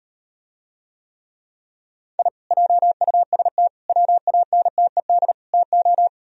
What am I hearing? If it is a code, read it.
Morse code, heard as a keyed tone, about 38 wpm, 700 Hz. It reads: I JUST WANTED TO